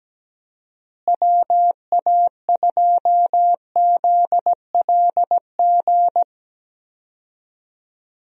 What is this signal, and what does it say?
Morse code, heard as a keyed tone, about 17 wpm, 700 Hz. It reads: WA2ZLG
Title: USA Call Signs - 6 Characters - Repeated 3x - 17wpm